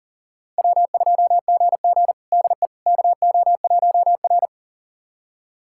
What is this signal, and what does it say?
Morse code, heard as a keyed tone, about 40 wpm, 700 Hz. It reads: W2GG DE KO1R
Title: USA Callsign DE Callsign - 4 Characters - 40wpm